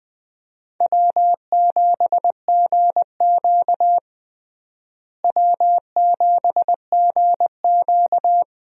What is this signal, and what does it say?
Morse code, heard as a keyed tone, about 21 wpm, 700 Hz. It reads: W7GQ W7GQ
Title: USA Call Signs - 4 Characters - Repeated 3x - 20wpm